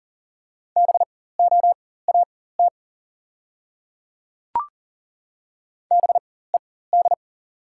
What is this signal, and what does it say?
Morse code, heard as a keyed tone, about 40 wpm, spaced as for 19 wpm, 690 Hz. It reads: BOAT BED
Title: 4x Character Spacing - Top 500 Words - 40wpm